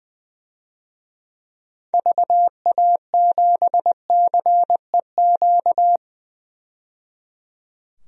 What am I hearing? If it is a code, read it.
Morse code, heard as a keyed tone, about 20 wpm, 700 Hz. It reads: VA7CEQ
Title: DXCC-100 Call Signs - 6 Characters - 20wpm